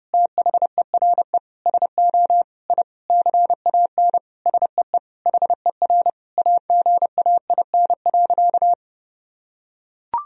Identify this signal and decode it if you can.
Morse code, heard as a keyed tone, about 30 wpm, 700 Hz. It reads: THERE SO I CAN SEE HER AGAIN.
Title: Sentences from Top 1000 Words - Review Part 3 - 30wpm